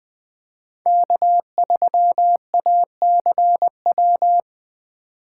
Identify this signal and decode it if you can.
Morse code, heard as a keyed tone, about 20 wpm, 700 Hz. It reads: K3ACW